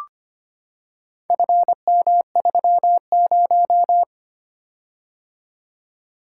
Morse code, heard as a keyed tone, about 25 wpm, 700 Hz. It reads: FM30
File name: practice / 25wpm / Four Letter-Number Combinations - 25wpm